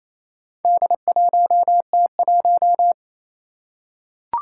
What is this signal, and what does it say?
Morse code, heard as a keyed tone, about 27 wpm, 700 Hz. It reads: D1T1
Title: Four Challenging Letter-Number Combinations 28wpm